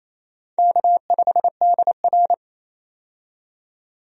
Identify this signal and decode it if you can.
Morse code, heard as a keyed tone, about 28 wpm, 700 Hz. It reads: K5DR
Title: USA Call Signs - 4 Characters - 28wpm